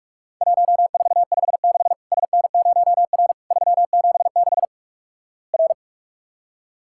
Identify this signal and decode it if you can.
Morse code, heard as a keyed tone, about 46 wpm, 700 Hz. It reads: 1456 SN0R 376 R
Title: Worked All Europe DX Contest - QTC Exchange - 45wpm